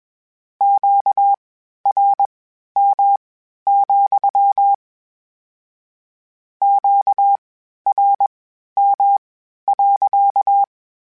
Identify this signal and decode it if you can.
Morse code, heard as a keyed tone, about 21 wpm, spaced as for 12 wpm, 800 Hz. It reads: QRM, QRM.
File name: Q Calls / Q Calls-mp3s / Q Calls sppc @20